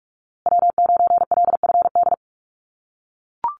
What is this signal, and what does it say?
Morse code, heard as a keyed tone, about 44 wpm, 700 Hz. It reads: W9LFD
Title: USA Call Signs Repeated 2x 45wpm